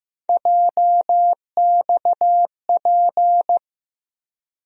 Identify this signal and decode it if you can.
Morse code, heard as a keyed tone, about 15 wpm, 700 Hz. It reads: JXP
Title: USA Call Sign Postfixes - 3 Letters 15wpm